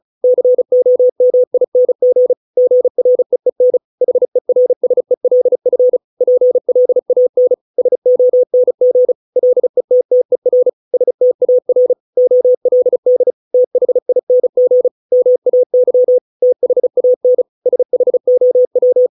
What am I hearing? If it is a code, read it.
Morse code, heard as a keyed tone, about 35 wpm, 500 Hz. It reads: COMING GREEN HERSELF PLAN SONG LETTER STAR OLD THING MAY THAN SHOW